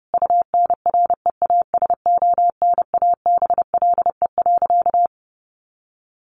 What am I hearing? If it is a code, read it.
Morse code, heard as a keyed tone, about 30 wpm, 700 Hz. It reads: UNREASONABLE.
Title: A Sound Of Thunder 30wpm